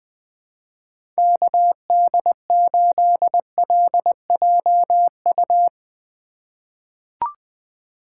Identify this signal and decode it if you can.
Morse code, heard as a keyed tone, about 20 wpm, 700 Hz. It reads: KD8LJU
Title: USA Call Signs - 6 Characters - 2x Character Spacing - 20wpm